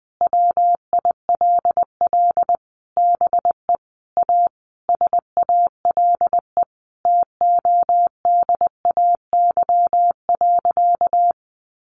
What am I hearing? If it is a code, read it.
Morse code, heard as a keyed tone, about 20 wpm, 700 Hz. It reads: WILL BE A SALE TODAY.
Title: Sentences from Top 1400 Words 20wpm